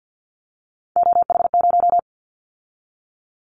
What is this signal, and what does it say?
Morse code, heard as a keyed tone, about 50 wpm, 700 Hz. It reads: OH0
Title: Callsign Prefixes for Top 100 DXCC - 50wpm